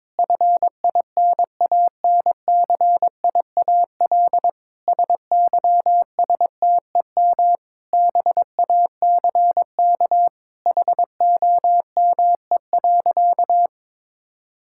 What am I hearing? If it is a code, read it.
Morse code, heard as a keyed tone, about 22 wpm, 700 Hz. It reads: FINANCIAL SYSTEM BACK HOME.
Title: Sentences from Top 1000 Words - Review Part 5 - 22wpm